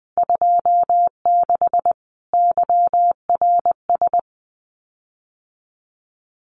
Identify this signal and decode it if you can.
Morse code, heard as a keyed tone, about 20 wpm, 700 Hz. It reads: 26 YRS
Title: Common QSO Phrases Encore 20wpm